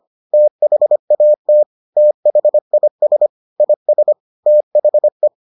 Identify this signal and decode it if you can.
Morse code, heard as a keyed tone, about 25 wpm, 600 Hz. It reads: THAT THIS IS THE